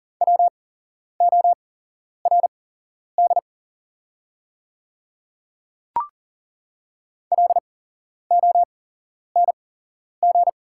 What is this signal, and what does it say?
Morse code, heard as a keyed tone, about 40 wpm, spaced as for 11 wpm, 700 Hz. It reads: WORD LONG
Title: 8x Character Spacing - Top 200 Words - 40wpm